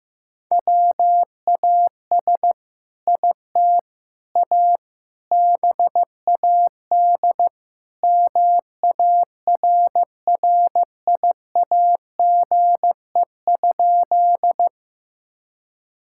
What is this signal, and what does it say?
Morse code, heard as a keyed tone, about 15 wpm, 700 Hz. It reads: WAS IT A BAD MARRIAGE?
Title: Sentences from Top 1000 Words - Review Part 5 - 15wpm